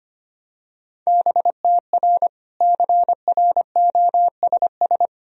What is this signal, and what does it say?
Morse code, heard as a keyed tone, about 25 wpm, 700 Hz. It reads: BTR CROSS
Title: Sentences from Top 400 Words and Abbreviations 25wpm